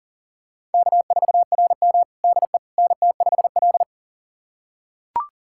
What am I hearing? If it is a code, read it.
Morse code, heard as a keyed tone, about 37 wpm, 700 Hz. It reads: K4RM DE NT5L
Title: USA Callsign DE Callsign - 4 Characters - 40wpm